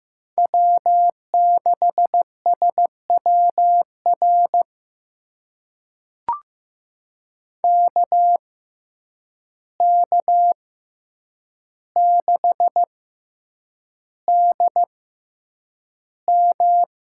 Morse code, heard as a keyed tone, about 15 wpm, spaced as for 8 wpm, 700 Hz. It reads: W6SWR KK6DM
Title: USA Call Signs - 5 Characters - 6x Character Spacing - 15wpm